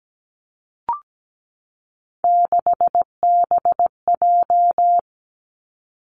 Morse code, heard as a keyed tone, about 17 wpm, 700 Hz. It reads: 6BJ
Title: Three Challenging Letter-Number Combinations 17wpm